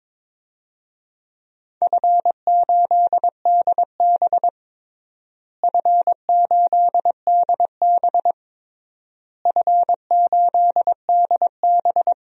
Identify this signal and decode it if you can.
Morse code, heard as a keyed tone, about 22 wpm, 700 Hz. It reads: F8DB F8DB F8DB
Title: DXCC-100 Call Signs - 4 Characters - Repeated 3x - 22wpm